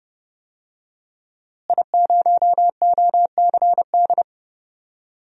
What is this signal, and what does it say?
Morse code, heard as a keyed tone, about 32 wpm, 700 Hz. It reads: I0OCD
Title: DXCC-100 Call Signs - 5 Characters - Repeated 2x - 30wpm